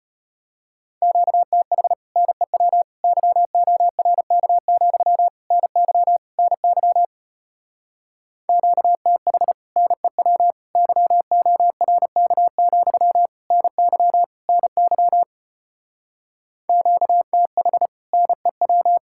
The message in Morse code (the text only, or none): QTH NEW YORK, NY NY QTH NEW YORK, NY NY QTH NEW